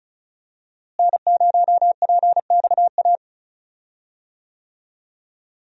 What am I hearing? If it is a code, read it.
Morse code, heard as a keyed tone, about 34 wpm, 700 Hz. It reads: N0PXA
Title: USA Call Signs - 5 Characters - Repeated 3x - 35wpm